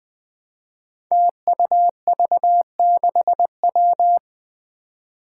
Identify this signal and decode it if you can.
Morse code, heard as a keyed tone, about 20 wpm, 700 Hz. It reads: TUV6W
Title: Five Challenging Letter-Number Combinations 20wpm